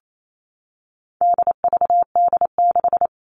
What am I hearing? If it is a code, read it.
Morse code, heard as a keyed tone, about 28 wpm, 700 Hz. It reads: DVD6
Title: All Cut Numbers - 4-Digit 28wpm